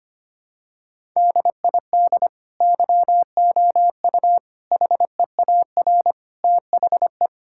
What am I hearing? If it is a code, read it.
Morse code, heard as a keyed tone, about 25 wpm, 700 Hz. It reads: DID YOU HEAR THE
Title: Sentences from Top 1000 Words - Review Part 5 - 25wpm